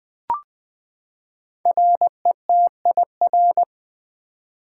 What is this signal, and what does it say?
Morse code, heard as a keyed tone, about 20 wpm, 700 Hz. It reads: RETIR
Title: Build A Word - 5 Letters At A Time - 10-Letter Words - 20wpm